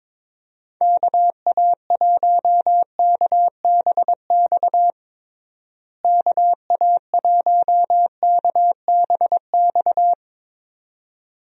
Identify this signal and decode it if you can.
Morse code, heard as a keyed tone, about 22 wpm, 700 Hz. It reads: KA1KBX KA1KBX
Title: USA Call Signs - 6 Characters - Repeated 2x - 22wpm